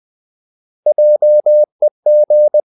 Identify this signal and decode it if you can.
Morse code, heard as a keyed tone, about 20 wpm, 600 Hz. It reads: JEG
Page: Ferie 20 wpm